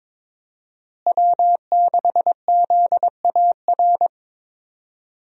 Morse code, heard as a keyed tone, about 22 wpm, 700 Hz. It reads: W6ZAR
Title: USA Call Signs - 5 Characters - Repeated 2x - 22wpm